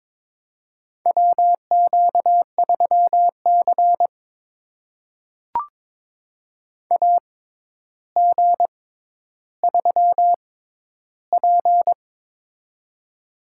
Morse code, heard as a keyed tone, about 22 wpm, spaced as for 11 wpm, 700 Hz. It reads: WQ3C AG3P
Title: USA Call Signs - 4 Characters - 6x Character Spacing - 22wpm